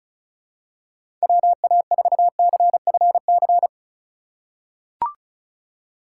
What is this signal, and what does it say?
Morse code, heard as a keyed tone, about 36 wpm, 700 Hz. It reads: WA4CFC
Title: USA Call Signs - 6 Characters - 2x Character Spacing - 35wpm